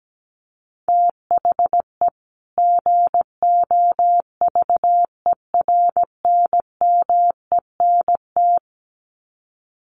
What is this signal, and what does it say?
Morse code, heard as a keyed tone, about 17 wpm, 700 Hz. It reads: THE GOVERNMENT
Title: Sets of 2 Words 17wpm